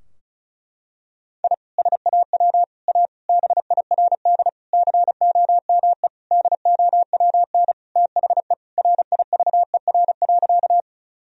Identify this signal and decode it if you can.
Morse code, heard as a keyed tone, about 36 wpm, 700 Hz. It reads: I SAW A BIRD COME DOWN THE RIVER.
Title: Sentences from Top 1000 Words - Review Part 2 - 35wpm